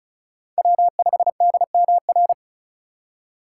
Morse code, heard as a keyed tone, about 36 wpm, 700 Hz. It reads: W5DMR
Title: USA Call Signs - 2x Character Spacing - 35wpm